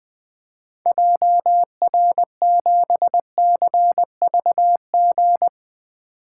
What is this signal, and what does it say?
Morse code, heard as a keyed tone, about 20 wpm, 700 Hz. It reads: JR7CVG
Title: 2023 POTA Activator Call Signs - 20wpm